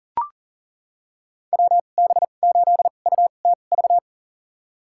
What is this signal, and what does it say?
Morse code, heard as a keyed tone, about 40 wpm, spaced as for 29 wpm, 700 Hz. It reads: WB8UTV
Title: USA Call Signs - 6 Characters - 2x Character Spacing - 40wpm